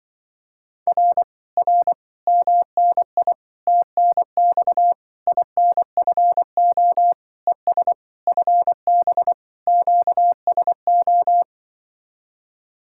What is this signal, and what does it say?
Morse code, heard as a keyed tone, about 25 wpm, 700 Hz. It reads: R R MNI TNX INFO ES FB QSO